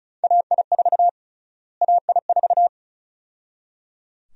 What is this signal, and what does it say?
Morse code, heard as a keyed tone, about 36 wpm, 700 Hz. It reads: AI4 AI4
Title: USA Call Sign Prefixes - Repeated 2x - 35wpm